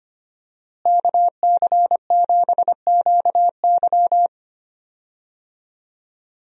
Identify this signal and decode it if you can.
Morse code, heard as a keyed tone, about 25 wpm, 700 Hz. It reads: KC7QY
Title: USA Call Signs - 5 Characters - Repeated 2x - 25wpm